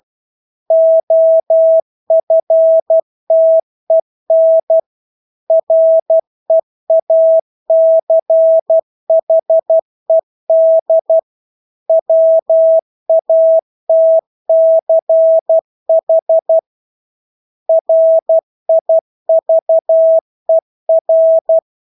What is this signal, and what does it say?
Morse code, heard as a keyed tone, about 12 wpm, 650 Hz. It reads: OFTEN REACHED WATCH RIVER
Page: Engelske ord 12 wpm